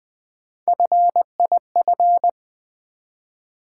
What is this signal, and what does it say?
Morse code, heard as a keyed tone, about 20 wpm, 700 Hz. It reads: FIF
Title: Build A Word - 1 Letter At A Time - 5-Letter Words - 20wpm